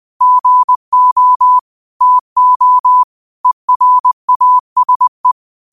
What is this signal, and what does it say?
Morse code, heard as a keyed tone, about 20 wpm, 1000 Hz. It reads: GO TO ERASE
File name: qotdfiles / 20wpm